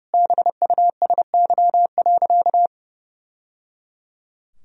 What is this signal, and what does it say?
Morse code, heard as a keyed tone, about 30 wpm, 700 Hz. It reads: BUSY.
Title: Sentences from Top 1000 Words - Review Part 3 - 30wpm